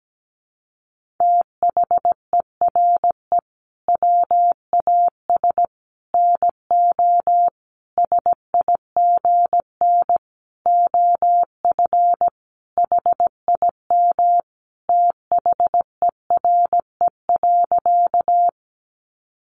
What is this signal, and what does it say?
Morse code, heard as a keyed tone, about 17 wpm, 700 Hz. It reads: THERE WAS NO SIGN OF HIM THERE.
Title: Sentences from Top 1000 Words 17wpm